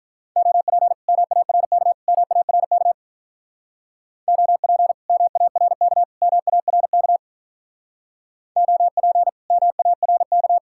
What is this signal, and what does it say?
Morse code, heard as a keyed tone, about 43 wpm, 700 Hz. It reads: OP MARK MARK OP MARK MARK OP MARK
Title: Speed-Racing Common QSO Phrases - 35wpm